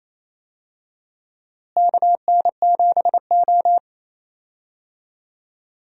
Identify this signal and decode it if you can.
Morse code, heard as a keyed tone, about 28 wpm, 700 Hz. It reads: KN7O
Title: USA Call Signs - 4 Characters - Repeated 2x - 28wpm